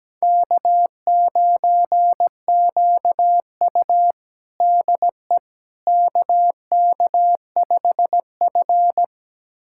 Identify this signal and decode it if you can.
Morse code, heard as a keyed tone, about 17 wpm, 700 Hz. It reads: K9QU DE KK5F
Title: USA Callsign DE Callsign - 4 Characters - 17wpm